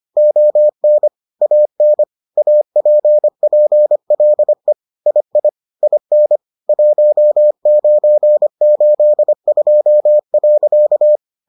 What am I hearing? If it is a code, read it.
Morse code, heard as a keyed tone, about 25 wpm, 600 Hz. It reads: ON AN APPLE II IN 1982.